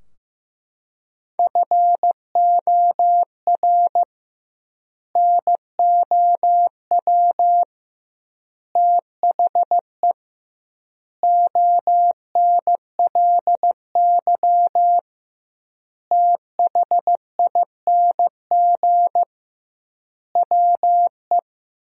Morse code, heard as a keyed tone, about 15 wpm, 700 Hz. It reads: FOR NOW THE ONLY THING WE
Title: 2x Word-Spacing - Sentences from Top 200 Words - 15wpm